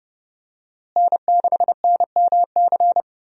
Morse code, heard as a keyed tone, about 30 wpm, 700 Hz. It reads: N6NMC
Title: USA Call Signs - 5 Characters - Repeated 2x - 30wpm